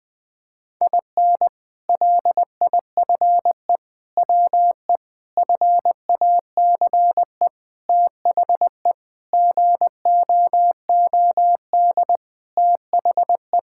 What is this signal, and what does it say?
Morse code, heard as a keyed tone, about 20 wpm, 700 Hz. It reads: IN LIFE WE FACE THE GOOD THE
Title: Sentences with 7-Word Phrases - 20wpm